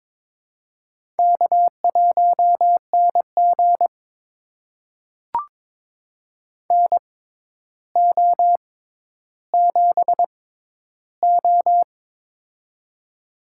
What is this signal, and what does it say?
Morse code, heard as a keyed tone, about 21 wpm, spaced as for 11 wpm, 700 Hz. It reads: K1NG NO7O
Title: USA Call Signs - 4 Characters - 6x Character Spacing - 22wpm